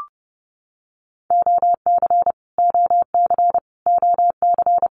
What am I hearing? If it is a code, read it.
Morse code, heard as a keyed tone, about 30 wpm, 700 Hz. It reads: OC OC OC
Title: Abbreviations Repeated 3x - 30wpm